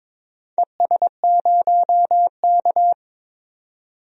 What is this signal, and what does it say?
Morse code, heard as a keyed tone, about 22 wpm, 700 Hz. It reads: ES0K
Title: DXCC-100 Call Signs - 4 Characters - 22wpm